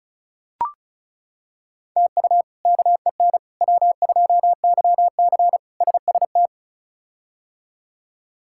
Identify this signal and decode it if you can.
Morse code, heard as a keyed tone, about 36 wpm, 700 Hz. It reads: TU KEN W2YC SST
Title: K1USN SST QSO Practice - 35wpm